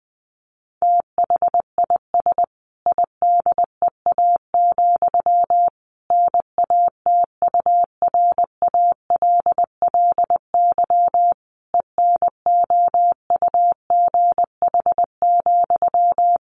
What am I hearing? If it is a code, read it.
Morse code, heard as a keyed tone, about 19 wpm, 700 Hz. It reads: THIS IDEA, NATURALLY ENOUGH,
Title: Sentences from Top 1300 Words 20wpm